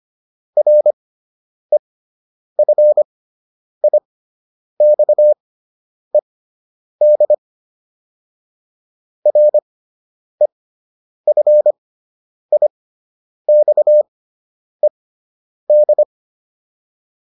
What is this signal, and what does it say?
Morse code, heard as a keyed tone, about 25 wpm, spaced as for 9 wpm, 600 Hz. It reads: REFIXED REFIXED